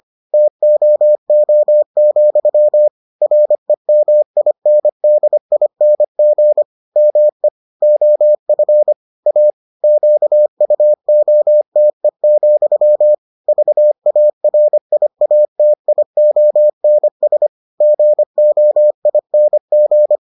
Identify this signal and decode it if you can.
Morse code, heard as a keyed tone, about 25 wpm, 600 Hz. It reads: TOO, REMINDING ME OF A QUOTE, VARIATIONS GOING